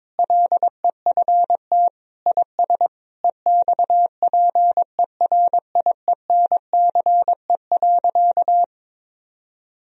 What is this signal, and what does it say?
Morse code, heard as a keyed tone, about 22 wpm, 700 Hz. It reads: LEFT IS EXPERIENCE.
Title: Sentences from Top 1000 Words - Review Part 3 - 22wpm